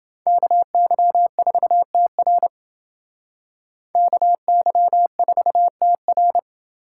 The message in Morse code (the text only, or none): KY4TR KY4TR